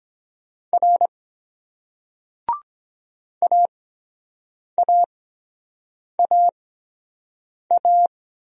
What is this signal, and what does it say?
Morse code, heard as a keyed tone, about 23 wpm, spaced as for 7 wpm, 700 Hz. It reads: R AAAA